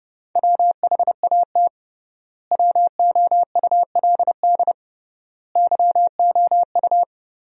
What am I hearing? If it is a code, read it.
Morse code, heard as a keyed tone, about 32 wpm, 700 Hz. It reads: WHAT WOULD YOU
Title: 3x Word-Spacing - Sentences from Top 300 Words - 30wpm